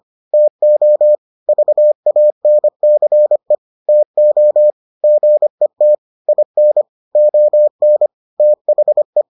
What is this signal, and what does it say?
Morse code, heard as a keyed tone, about 25 wpm, 600 Hz. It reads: TO VANCE TO GET IN ON THE